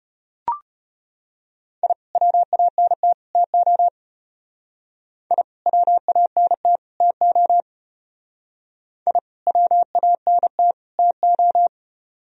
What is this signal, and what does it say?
Morse code, heard as a keyed tone, about 33 wpm, 700 Hz. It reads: I WANT TO I WANT TO I WANT TO